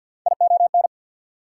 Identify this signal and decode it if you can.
Morse code, heard as a keyed tone, about 68 wpm, spaced as for 59 wpm, 700 Hz. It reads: ION
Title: Speed-Racing Prefixes and Suffixes 50wpm